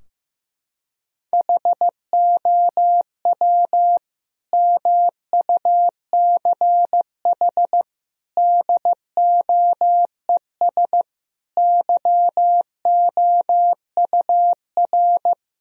Morse code, heard as a keyed tone, about 15 wpm, 700 Hz. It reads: HOW MUCH DOES YOUR
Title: Sentences from Top 1000 Words - Review Part 4 - 15wpm